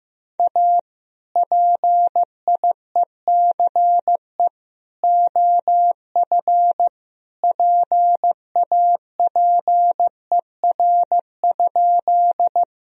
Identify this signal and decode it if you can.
Morse code, heard as a keyed tone, about 15 wpm, 700 Hz. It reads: A PIECE OF PAPER?
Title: Sentences from Top 1000 Words - Review Part 2 - 15wpm